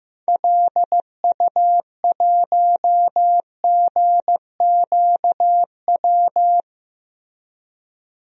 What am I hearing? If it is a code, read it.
Morse code, heard as a keyed tone, about 16 wpm, 700 Hz. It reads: LU1GQW
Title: DXCC-100 Call Signs - 6 Characters - Repeated 2x - 15wpm